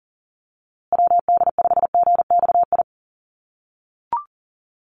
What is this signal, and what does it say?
Morse code, heard as a keyed tone, about 40 wpm, 700 Hz. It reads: WD5GXI
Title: USA Call Signs Repeated 2x 40wpm